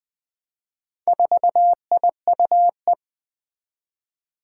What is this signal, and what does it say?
Morse code, heard as a keyed tone, about 20 wpm, 700 Hz. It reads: 4IUE
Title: Four Challenging Letter-Number Combinations 20wpm